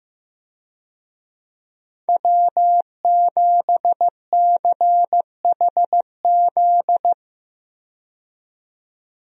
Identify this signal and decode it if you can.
Morse code, heard as a keyed tone, about 15 wpm, 700 Hz. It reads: W7CHZ
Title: USA Call Signs - 5 Characters - 15wpm